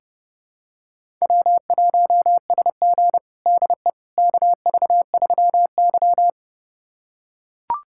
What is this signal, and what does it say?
Morse code, heard as a keyed tone, about 30 wpm, 700 Hz. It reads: W1SG DE KV3Y
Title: USA Callsign DE Callsign - 4 Characters - 30wpm